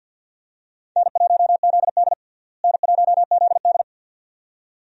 Morse code, heard as a keyed tone, about 50 wpm, 700 Hz. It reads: N1ZD N1ZD
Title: USA Call Signs - 4 Characters - Repeated 2x - 50wpm